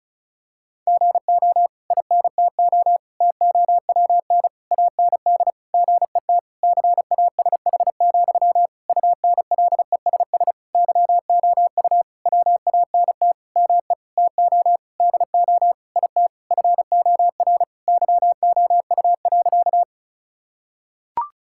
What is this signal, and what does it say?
Morse code, heard as a keyed tone, about 34 wpm, 700 Hz. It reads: GO INTO TOWN AND GET CASH, UNLESS YOU WANT ME TO DO IT FOR YOU.
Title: Sentences from Top 1000 Words - Review Part 6 - 35wpm